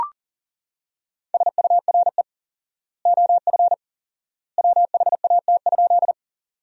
Morse code, heard as a keyed tone, about 40 wpm, 700 Hz. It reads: SURE OF WHAT?
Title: 4x Word-Spacing - Sentences from Top 300 Words - 40wpm